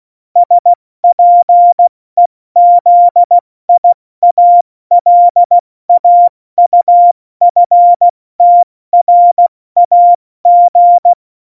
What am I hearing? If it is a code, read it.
Morse code, heard as a keyed tone, about 16 wpm, 700 Hz. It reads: SPEZIALAUFTRAG
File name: Morsekurs / Witze / Witz 12 Mai, 2020 70 F80